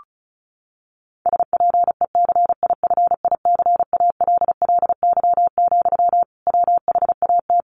Morse code, heard as a keyed tone, about 34 wpm, 700 Hz. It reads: SPECIFICALLY, WHAT
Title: Sentences from Top 1100 Words 35wpm